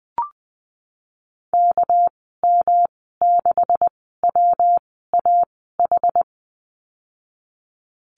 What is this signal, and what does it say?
Morse code, heard as a keyed tone, about 20 wpm, spaced as for 14 wpm, 700 Hz. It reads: KM6WAH
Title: 213 Lesson 34g - Number 8 - Call Signs 20wpm